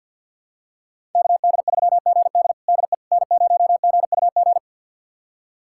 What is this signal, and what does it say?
Morse code, heard as a keyed tone, about 47 wpm, 700 Hz. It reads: KD3GD DE N0GUG
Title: USA Callsign DE Callsign - 5 Characters - 50wpm